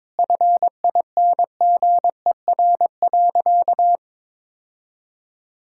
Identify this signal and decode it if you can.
Morse code, heard as a keyed tone, about 22 wpm, 700 Hz. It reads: FINGER.
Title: Sentences from Top 1600 Words 22wpm